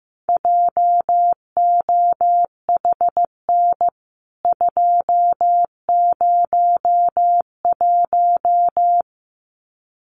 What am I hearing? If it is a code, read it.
Morse code, heard as a keyed tone, about 15 wpm, 700 Hz. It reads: JOHN 201
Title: CWT Exchange 15wpm